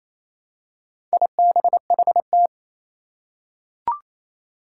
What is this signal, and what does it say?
Morse code, heard as a keyed tone, about 28 wpm, 700 Hz. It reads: IBHT
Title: Four Challenging Letter-Number Combinations 28wpm